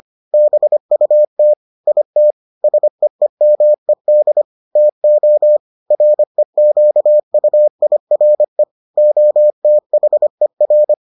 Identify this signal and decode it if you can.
Morse code, heard as a keyed tone, about 25 wpm, 600 Hz. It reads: BUT IT SEEMED TO REQUIRE OTHER